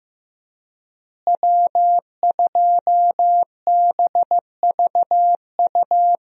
Morse code, heard as a keyed tone, about 16 wpm, 700 Hz. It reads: W2BVU
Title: USA Call Signs - 5 Characters - Repeated 2x - 15wpm